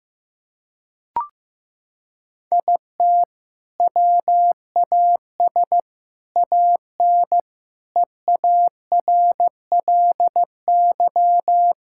Morse code, heard as a keyed tone, about 15 wpm, 700 Hz. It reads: IT WAS AN EARLY
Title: Sentences from Top 1000 Words - Review Part 5 - 15wpm